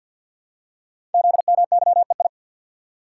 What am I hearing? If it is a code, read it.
Morse code, heard as a keyed tone, about 51 wpm, 700 Hz. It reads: ZMYEI
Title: Code Groups - 5 Letters - 50wpm